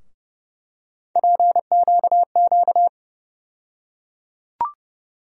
Morse code, heard as a keyed tone, about 30 wpm, 700 Hz. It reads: PQQ
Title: USA Call Sign Postfixes - 3 Letters 30wpm